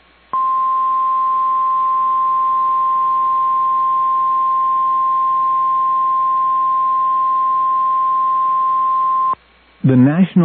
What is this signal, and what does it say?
Emergency Alert System (SAME) tones (the 1050 Hz attention tone). It is heard from 0.3 to 9.4 s.